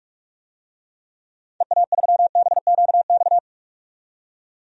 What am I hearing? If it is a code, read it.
Morse code, heard as a keyed tone, about 45 wpm, 700 Hz. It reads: EA3BQX
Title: DXCC-100 Call Signs - 6 Characters - 45wpm